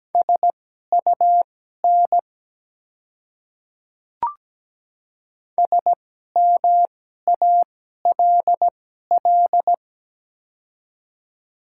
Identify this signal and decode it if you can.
Morse code, heard as a keyed tone, about 17 wpm, spaced as for 12 wpm, 700 Hz. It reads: SUN SMALL
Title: 2x Character Spacing - Top 300 Words - 17wpm